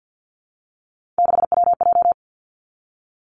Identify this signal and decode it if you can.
Morse code, heard as a keyed tone, about 50 wpm, 700 Hz. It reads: 6WJ
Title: Three Challenging Letter-Number Combinations 50wpm